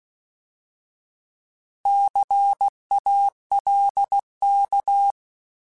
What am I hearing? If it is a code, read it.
Morse code, heard as a keyed tone, about 16 wpm, 800 Hz. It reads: CALK
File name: Alphabetical / Alphabetical mp3s / Cs @15